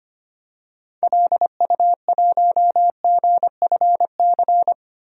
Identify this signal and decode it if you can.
Morse code, heard as a keyed tone, about 25 wpm, 690 Hz. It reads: LU1GFC
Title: DXCC-100 Call Signs - 6 Characters - 25wpm